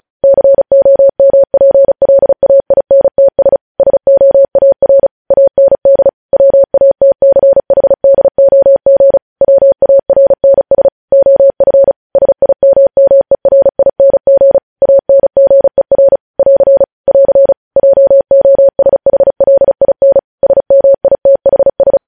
Morse code, heard as a keyed tone, about 36 wpm, 550 Hz. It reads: COMPLAINTS SOAR AND WATCHDOG WARNS OF SIMMERING ANGER <AR> <AR> JOSHLIN SMITHS
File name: podcast / News headlines-1x-35x35